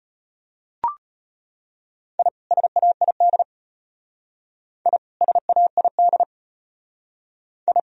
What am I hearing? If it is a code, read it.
Morse code, heard as a keyed tone, about 33 wpm, 700 Hz. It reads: I SAID I SAID I